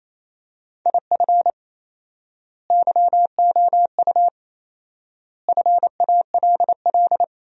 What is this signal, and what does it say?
Morse code, heard as a keyed tone, about 28 wpm, 700 Hz. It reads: IF YOU FALL
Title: 4x Word-Spacing - Sentences from Top 500 Words - 28wpm